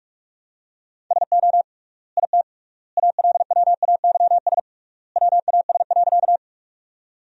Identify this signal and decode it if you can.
Morse code, heard as a keyed tone, about 45 wpm, 700 Hz. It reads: SO IT ALWAYS WAS.
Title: 3x Word-Spacing - Sentences from Top 300 Words - 45wpm